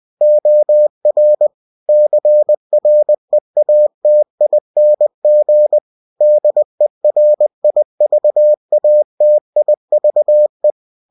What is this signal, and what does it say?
Morse code, heard as a keyed tone, about 20 wpm, 600 Hz. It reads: OR CREATING DERIVATIVE
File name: cw-books / winnie-the-pooh / winnie-20wpm-page-0114